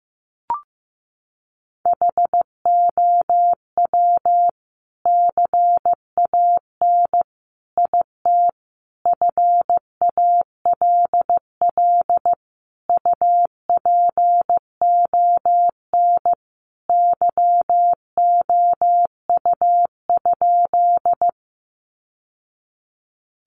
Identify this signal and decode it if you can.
Morse code, heard as a keyed tone, about 15 wpm, 700 Hz. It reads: HOW CAN IT FALL UPON YOU?
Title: Sentences from Top 1100 Words 15wpm